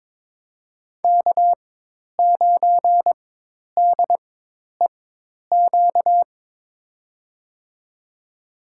Morse code, heard as a keyed tone, about 22 wpm, spaced as for 10 wpm, 700 Hz. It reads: K9DEQ
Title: USA Call Signs - 5 Characters - 4x Character Spacing - 22wpm